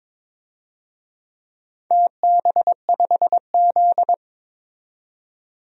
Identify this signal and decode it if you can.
Morse code, heard as a keyed tone, about 22 wpm, 700 Hz. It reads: TB5Z
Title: DXCC-100 Call Signs - 4 Characters - 22wpm